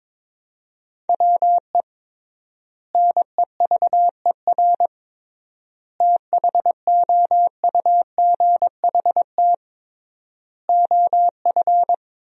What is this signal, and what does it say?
Morse code, heard as a keyed tone, about 22 wpm, 700 Hz. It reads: WE NEVER THOUGHT OF
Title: 3x Word-Spacing - Sentences from Top 300 Words - 22wpm